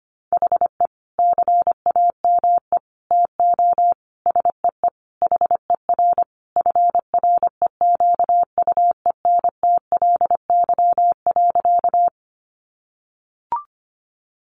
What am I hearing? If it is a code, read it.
Morse code, heard as a keyed tone, about 25 wpm, 700 Hz. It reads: HE CAME TO SEE HER FREQUENTLY.
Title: Sentences from Top 1300 Words 25wpm